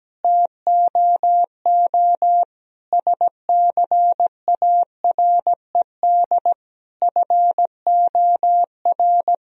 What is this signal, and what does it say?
Morse code, heard as a keyed tone, about 17 wpm, 700 Hz. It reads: TOO SCARED FOR
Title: Sentences from Top 1000 Words - Review Part 4 - 17wpm